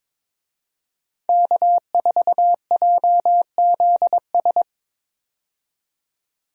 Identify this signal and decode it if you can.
Morse code, heard as a keyed tone, about 22 wpm, 700 Hz. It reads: K4JZS
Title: USA Call Signs - 5 Characters - 22wpm